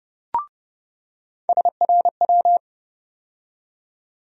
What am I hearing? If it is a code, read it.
Morse code, heard as a keyed tone, about 30 wpm, 700 Hz. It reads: SRW
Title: USA Call Sign Postfixes - 3 Letters 30wpm